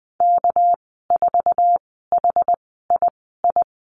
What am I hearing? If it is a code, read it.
Morse code, heard as a keyed tone, about 20 wpm, spaced as for 15 wpm, 700 Hz. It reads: K4HII
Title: 200 Lesson 32g - Letter k - Call Signs 20wpm